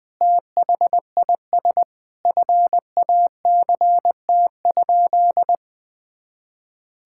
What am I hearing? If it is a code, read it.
Morse code, heard as a keyed tone, about 20 wpm, 700 Hz. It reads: THIS FACT?
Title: Sentences with 2-Word Phrases - 20wpm